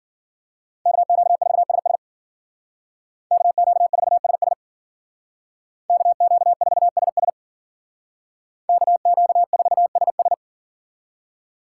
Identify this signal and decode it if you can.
Morse code, heard as a keyed tone, about 53 wpm, 700 Hz. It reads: KQ4SS KQ4SS KQ4SS KQ4SS